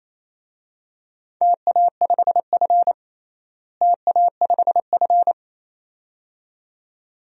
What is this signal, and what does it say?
Morse code, heard as a keyed tone, about 28 wpm, 700 Hz. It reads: TA5F TA5F
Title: DXCC-100 Call Signs - 4 Characters - Repeated 2x - 28wpm